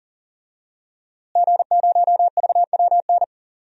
Morse code, heard as a keyed tone, about 37 wpm, 700 Hz. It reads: G0VWN
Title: DXCC-100 Call Signs - 5 Characters - Repeated 2x - 40wpm